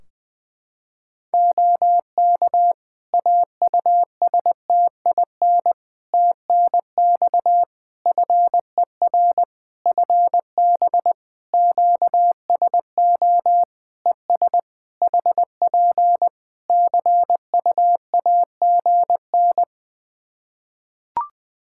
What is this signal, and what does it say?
Morse code, heard as a keyed tone, about 20 wpm, 700 Hz. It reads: OK AUSTIN TNX FER FB QSO ES HP CUAGN
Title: Practice QSO - Exchange 4 - Ending - 20wpm